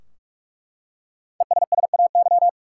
Morse code, heard as a keyed tone, about 47 wpm, 700 Hz. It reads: ESSAY
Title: Five Letter Words - 45wpm